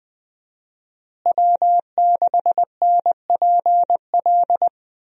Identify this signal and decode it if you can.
Morse code, heard as a keyed tone, about 20 wpm, 700 Hz. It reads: W6NPL
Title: USA Call Signs - 5 Characters - Repeated 3x - 20wpm